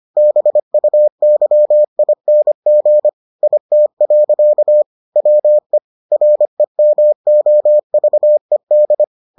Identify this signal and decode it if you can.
Morse code, heard as a keyed tone, about 25 wpm, 600 Hz. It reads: BUYING IT. WE REMOVED